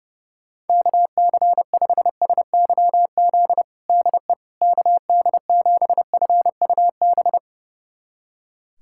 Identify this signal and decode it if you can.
Morse code, heard as a keyed tone, about 30 wpm, 700 Hz. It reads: KC5SYZ DE KD7FUB
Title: USA Callsign DE Callsign - 6 Characters - 30wpm